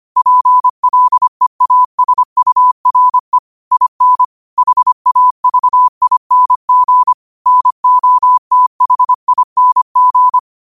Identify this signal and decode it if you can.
Morse code, heard as a keyed tone, about 25 wpm, 1000 Hz. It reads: PLEASURE IN HAVING NOTHING
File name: qotdfiles / 25wpm